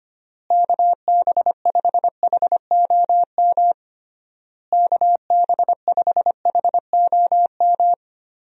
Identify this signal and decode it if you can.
Morse code, heard as a keyed tone, about 25 wpm, 700 Hz. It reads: KB5HOM KB5HOM
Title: USA Call Signs - 6 Characters - Repeated 3x - 25wpm